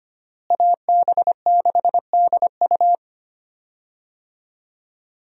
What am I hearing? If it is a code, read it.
Morse code, heard as a keyed tone, about 25 wpm, 700 Hz. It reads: AB6DU
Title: USA Call Signs - 5 Characters - Repeated 2x - 25wpm